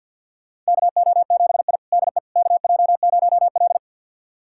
Morse code, heard as a keyed tone, about 50 wpm, 700 Hz. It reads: KO7I DE KJ0L
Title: USA Callsign DE Callsign - 4 Characters - 50wpm